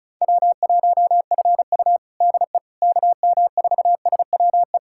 Morse code, heard as a keyed tone, about 34 wpm, 700 Hz. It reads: W1FU DE KM4SWE
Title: USA Callsign DE Callsign 35wpm